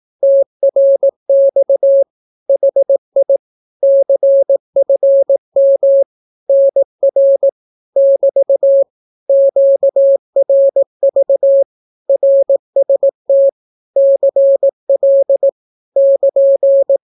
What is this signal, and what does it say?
Morse code, heard as a keyed tone, about 18 wpm, 550 Hz. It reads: TRX HI CFM NR <BT> QRV RST CL <KN>
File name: naukapodstaw / skroty / skroty